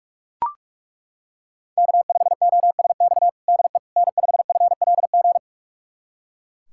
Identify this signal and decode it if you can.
Morse code, heard as a keyed tone, about 41 wpm, 700 Hz. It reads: K5OSX DE N5FLG
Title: USA Callsign DE Callsign - 5 Characters - 45wpm